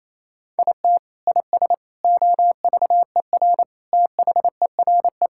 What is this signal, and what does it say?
Morse code, heard as a keyed tone, about 28 wpm, 700 Hz. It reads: IT IS OVER THERE
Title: Sentences from Top 1000 Words - Review Part 6 - 28wpm